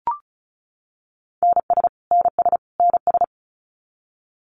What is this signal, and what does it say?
Morse code, heard as a keyed tone, about 34 wpm, 700 Hz. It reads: NS NS NS
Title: Canadian Provinces and Territories - Repeated 3x - 35wpm